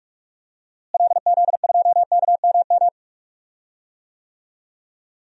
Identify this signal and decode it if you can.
Morse code, heard as a keyed tone, about 46 wpm, 700 Hz. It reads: LZ2KMM
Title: DXCC-100 Call Signs - 6 Characters - Repeated 2x - 45wpm